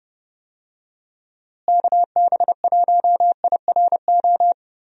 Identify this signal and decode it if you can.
Morse code, heard as a keyed tone, about 29 wpm, 700 Hz. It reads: KB1IRO
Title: USA Call Signs - 6 Characters - Repeated 3x - 30wpm